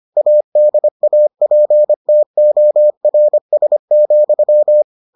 Morse code, heard as a keyed tone, about 25 wpm, 600 Hz. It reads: ADAPTORS,